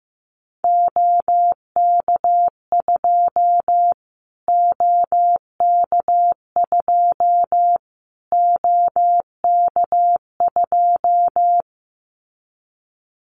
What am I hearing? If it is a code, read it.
Morse code, heard as a keyed tone, about 15 wpm, 700 Hz. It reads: OK2 OK2 OK2
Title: Callsign Prefixes for Top 25 DXCC - Repeated 3x - 15wpm